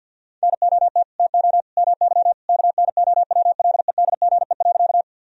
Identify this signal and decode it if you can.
Morse code, heard as a keyed tone, about 51 wpm, 700 Hz. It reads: NOT TO MY KNOWLEDGE.
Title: Sentences from Top 1000 Words - Review Part 3 - 50wpm